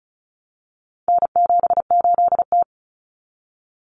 Morse code, heard as a keyed tone, about 34 wpm, 700 Hz. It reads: N78T
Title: Common Cut Numbers - 4-Digit 35wpm